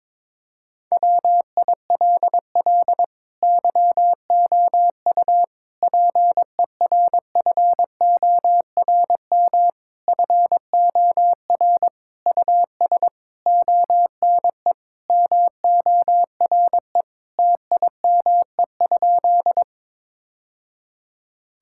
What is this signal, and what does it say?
Morse code, heard as a keyed tone, about 22 wpm, 700 Hz. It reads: WILL YOU PERFORM FOR US ONE MORE TIME?
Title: Sentences from Top 1000 Words - Review Part 5 - 22wpm